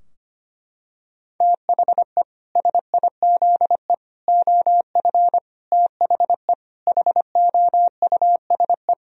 Morse code, heard as a keyed tone, about 25 wpm, 700 Hz. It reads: THE SIZE OF THE HOUSE
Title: Sentences from Top 1000 Words - Review Part 2 - 25wpm